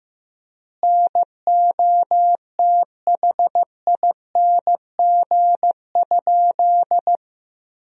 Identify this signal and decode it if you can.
Morse code, heard as a keyed tone, about 16 wpm, 700 Hz. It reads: NOTHING?
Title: 3x Word-Spacing - Sentences from Top 500 Words - 15wpm